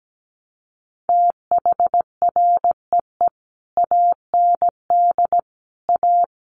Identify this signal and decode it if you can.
Morse code, heard as a keyed tone, about 17 wpm, 700 Hz. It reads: THREE AND A
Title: Sets of 5 Words Encore2 17wpm